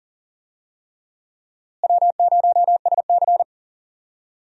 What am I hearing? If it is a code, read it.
Morse code, heard as a keyed tone, about 40 wpm, 700 Hz. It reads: W0SC
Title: USA Call Signs - 4 Characters - Repeated 3x - 40wpm